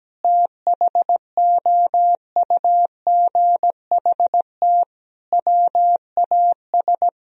Morse code, heard as a keyed tone, about 17 wpm, 700 Hz. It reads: THOUGHT WAS
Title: Sentences from Top 1000 Words - Review Part 4 - 17wpm